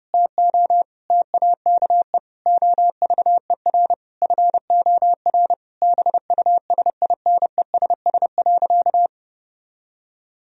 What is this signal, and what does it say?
Morse code, heard as a keyed tone, about 31 wpm, 700 Hz. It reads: TO TAKE OVER FOR BUSINESS.
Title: Sentences from Top 1000 Words - Review Part 3 - 30wpm